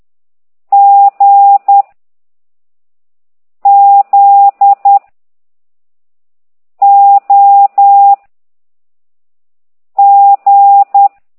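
Morse code, heard as a keyed tone, about 10 wpm, 800 Hz. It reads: G Z O G
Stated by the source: Morse code lesson 5 exercise 10 by CT2IDW, containing only characters M, O, G, Z, Q and comma.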